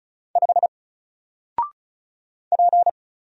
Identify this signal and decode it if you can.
Morse code, heard as a keyed tone, about 34 wpm, 700 Hz. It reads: 5 P
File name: Single-Letter-Number-35wpm